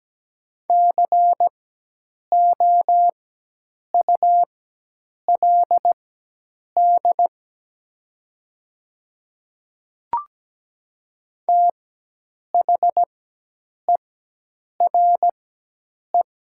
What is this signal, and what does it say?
Morse code, heard as a keyed tone, about 17 wpm, spaced as for 8 wpm, 700 Hz. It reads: COULD THERE
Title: 4x Character Spacing - Top 100 Words - 17wpm